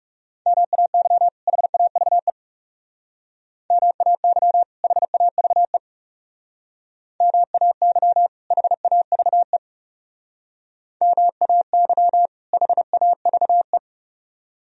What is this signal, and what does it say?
Morse code, heard as a keyed tone, about 37 wpm, 700 Hz. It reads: MAY HAVE MAY HAVE MAY HAVE MAY HAVE
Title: Speed-Racing - Sets of 2 Words 30wpm